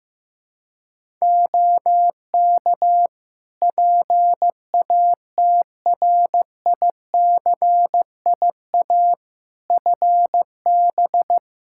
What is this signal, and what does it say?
Morse code, heard as a keyed tone, about 15 wpm, 700 Hz. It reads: OK PATRICIA FB
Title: Practice QSO - Exchange 2 - Info - 15wpm